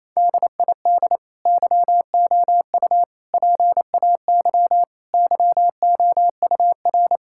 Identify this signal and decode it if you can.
Morse code, heard as a keyed tone, about 28 wpm, 700 Hz. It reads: DID YOU PAY YOUR
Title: Sentences from Top 1000 Words - Review Part 5 - 28wpm